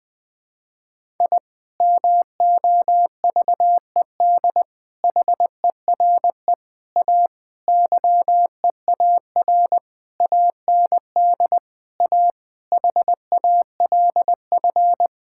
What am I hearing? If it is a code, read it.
Morse code, heard as a keyed tone, about 20 wpm, 700 Hz. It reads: I MOVED HERE A YEAR AND A HALF
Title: Sentences with 6-Word Phrases - Spoken Beginning - 20wpm